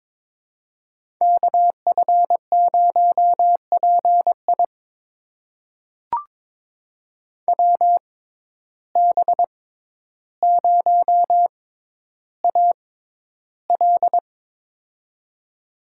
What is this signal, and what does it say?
Morse code, heard as a keyed tone, about 22 wpm, spaced as for 11 wpm, 700 Hz. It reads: KF0PI WB0AL
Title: USA Call Signs - 5 Characters - 6x Character Spacing - 22wpm